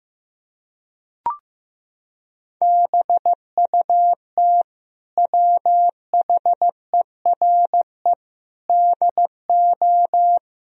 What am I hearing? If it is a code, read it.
Morse code, heard as a keyed tone, about 15 wpm, 700 Hz. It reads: BUT WHERE DO
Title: Sentences from Top 1000 Words - Review Part 1 - 15wpm